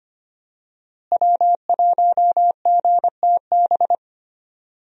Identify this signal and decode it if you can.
Morse code, heard as a keyed tone, about 25 wpm, 700 Hz. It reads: W1GTB
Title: USA Call Signs - 5 Characters - Repeated 2x - 25wpm